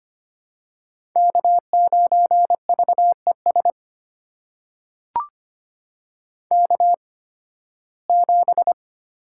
Morse code, heard as a keyed tone, about 25 wpm, spaced as for 16 wpm, 700 Hz. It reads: K9VES K7
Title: USA Call Signs - 5 Characters - 8x Character Spacing - 25wpm